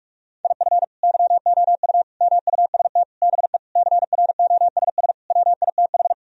Morse code, heard as a keyed tone, about 46 wpm, 700 Hz. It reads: IF YOU MUST BE CROSS WITH
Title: Sentences from Top 1000 Words - Review Part 1 - 45wpm